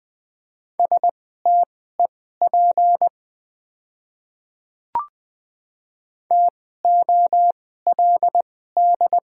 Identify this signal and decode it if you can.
Morse code, heard as a keyed tone, about 20 wpm, spaced as for 14 wpm, 690 Hz. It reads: STEP TOLD
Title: 2x Character Spacing - Top 500 Words - 20wpm